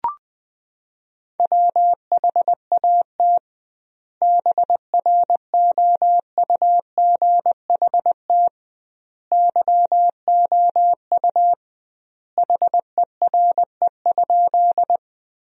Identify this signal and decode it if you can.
Morse code, heard as a keyed tone, about 20 wpm, 700 Hz. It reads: WHAT BROUGHT YOU HERE?
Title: 2x Word-Spacing - Sentences from Top 500 Words - 20wpm